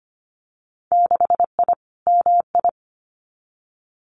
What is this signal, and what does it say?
Morse code, heard as a keyed tone, about 25 wpm, 700 Hz. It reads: 6I MI
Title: Winter Field Day Exchanges Encore 25wpm